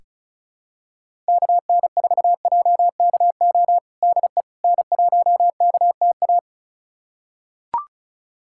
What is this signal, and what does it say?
Morse code, heard as a keyed tone, about 34 wpm, 700 Hz. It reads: KN4JKO DE N1KTA
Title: USA Callsign DE Callsign 35wpm